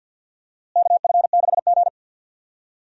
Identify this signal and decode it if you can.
Morse code, heard as a keyed tone, about 53 wpm, 700 Hz. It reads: KF6G
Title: USA Call Signs - 4 Characters - Repeated 3x - 50wpm